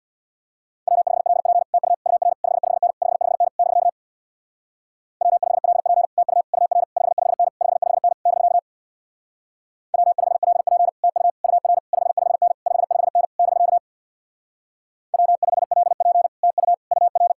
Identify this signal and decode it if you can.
Morse code, heard as a keyed tone, about 62 wpm, 700 Hz. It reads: W5LP TU UR 55N 55N <BK> W5LP TU UR 55N 55N <BK> W5LP TU UR 55N 55N <BK> W5LP TU UR
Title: Speed-Racing Common QSO Phrases - 50wpm